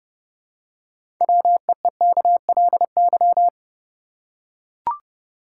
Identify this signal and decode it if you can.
Morse code, heard as a keyed tone, about 31 wpm, 700 Hz. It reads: WEEKLY
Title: Six Letter Words - 30wpm